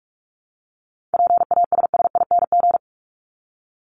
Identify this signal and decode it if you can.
Morse code, heard as a keyed tone, about 44 wpm, 700 Hz. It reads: PASSING
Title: ING Suffix - Top 100 Words with Base Words 45wpm